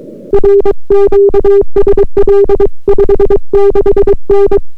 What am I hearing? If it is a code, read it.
Morse code, heard as a keyed tone, about 23 wpm, 400 Hz. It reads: R QSL 56N